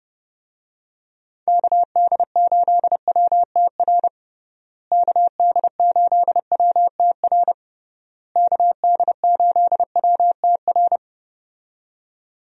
Morse code, heard as a keyed tone, about 30 wpm, 700 Hz. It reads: KD8WTR KD8WTR KD8WTR
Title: USA Call Signs - 6 Characters - Repeated 3x - 30wpm